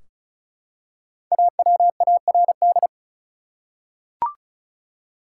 Morse code, heard as a keyed tone, about 36 wpm, 700 Hz. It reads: AWARD
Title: Five Letter Words - 35wpm